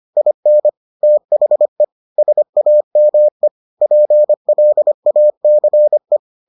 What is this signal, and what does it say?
Morse code, heard as a keyed tone, about 25 wpm, 600 Hz. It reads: IN THE SAME PLACE